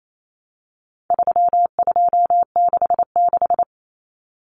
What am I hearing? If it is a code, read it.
Morse code, heard as a keyed tone, about 28 wpm, 700 Hz. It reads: 3266
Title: Numbers - 4-Digit 28wpm